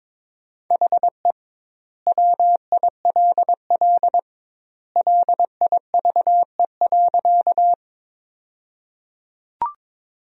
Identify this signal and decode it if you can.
Morse code, heard as a keyed tone, about 23 wpm, 700 Hz. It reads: HE WILL LIVE.
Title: 2x Word-Spacing - Sentences from Top 200 Words - 22wpm